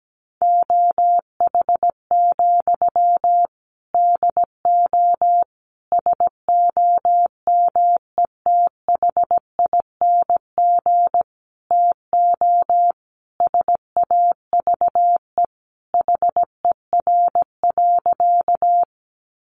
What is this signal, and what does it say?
Morse code, heard as a keyed tone, about 17 wpm, 700 Hz. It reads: OH, DO SOMETHING TO SAVE HER.
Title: Sentences from Top 1100 Words 17wpm